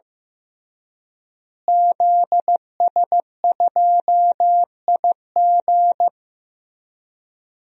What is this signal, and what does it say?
Morse code, heard as a keyed tone, about 16 wpm, 700 Hz. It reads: ZS2IG
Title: DXCC-100 Call Signs - 5 Characters - Repeated 3x - 15wpm